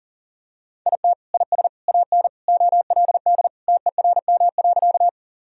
Morse code, heard as a keyed tone, about 40 wpm, 690 Hz. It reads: IT IS AN OLD TERM.
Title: Sentences from Top 1000 Words - Review Part 4 - 40wpm